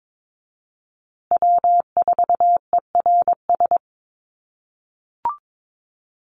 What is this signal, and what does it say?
Morse code, heard as a keyed tone, about 22 wpm, 700 Hz. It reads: W4ERS
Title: USA Call Signs Repeated 2x 22wpm